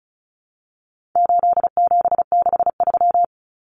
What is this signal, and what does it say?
Morse code, heard as a keyed tone, about 34 wpm, 700 Hz. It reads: 8763
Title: Common Cut Numbers - 4-Digit 35wpm